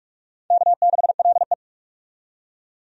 Morse code, heard as a keyed tone, about 42 wpm, 700 Hz. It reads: K6LE
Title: USA Call Signs - 4 Characters - 45wpm